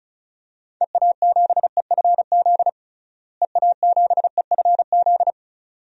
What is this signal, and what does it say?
Morse code, heard as a keyed tone, about 36 wpm, 700 Hz. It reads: EA7EFZ EA7EFZ
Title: DXCC-100 Call Signs - 6 Characters - Repeated 2x - 35wpm